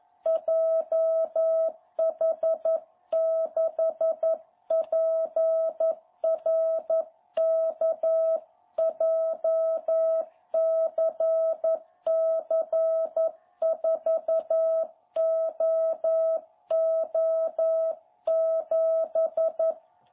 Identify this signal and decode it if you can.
Morse code, heard as a keyed tone, about 11 wpm, 660 Hz. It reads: JH6PRK JCC4OO7
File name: jh8cbh / 2021cw